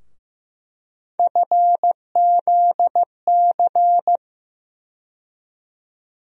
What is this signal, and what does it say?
Morse code, heard as a keyed tone, about 15 wpm, 700 Hz. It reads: FZC
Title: USA Call Sign Postfixes - 3 Letters 15wpm